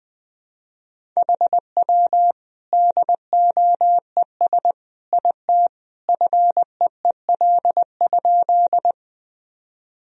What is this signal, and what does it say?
Morse code, heard as a keyed tone, about 20 wpm, 700 Hz. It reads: HW DOES IT FEEL?
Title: Sentences from Top 400 Words and Abbreviations 20wpm